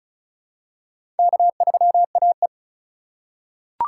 Morse code, heard as a keyed tone, about 36 wpm, 700 Hz. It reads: K3AE
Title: USA Call Signs - 4 Characters - 3x Character Spacing - 35wpm